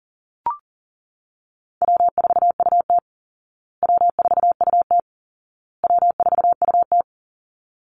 Morse code, heard as a keyed tone, about 40 wpm, 700 Hz. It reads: W4UT W4UT W4UT
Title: Easy Call Signs Repeated 3x - 40wpm